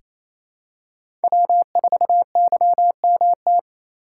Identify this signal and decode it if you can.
Morse code, heard as a keyed tone, about 28 wpm, 700 Hz. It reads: W4YMT
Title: USA Call Signs - 3x Character Spacing - 28wpm